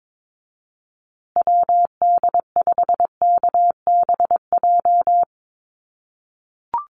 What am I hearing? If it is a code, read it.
Morse code, heard as a keyed tone, about 22 wpm, 700 Hz. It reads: WD5KBJ
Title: USA Call Signs Repeated 2x 22wpm